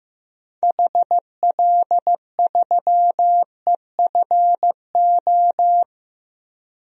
Morse code, heard as a keyed tone, about 15 wpm, 700 Hz. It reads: HL3EFO
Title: DXCC-100 Call Signs - 6 Characters - 15wpm